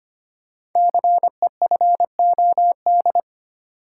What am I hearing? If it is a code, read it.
Morse code, heard as a keyed tone, about 26 wpm, 690 Hz. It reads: CEFOD
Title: Code Groups - 5 Letters - 25wpm